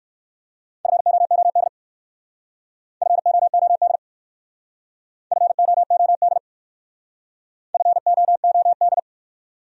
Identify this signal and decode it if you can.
Morse code, heard as a keyed tone, about 66 wpm, 700 Hz. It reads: FOOD FOOD FOOD FOOD